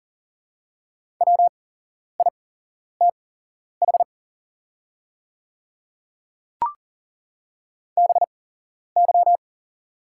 Morse code, heard as a keyed tone, about 40 wpm, spaced as for 11 wpm, 700 Hz. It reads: WITH BY
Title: 8x Character Spacing - Top 100 Words - 40wpm